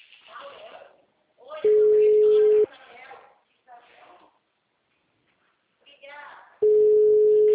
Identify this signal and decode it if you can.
Ringback tone (Europe, 425 Hz), with speech.
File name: out- -2010- - - .65685